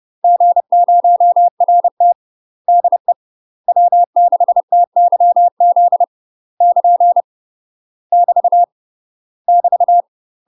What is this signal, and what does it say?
Morse code, heard as a keyed tone, about 30 wpm, 700 Hz. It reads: G0RT DE W6TYZ <KN> <BT> <BT>
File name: UK MASTER QSOs TRAINING / 30WPM / UK MASTER QSOs 0030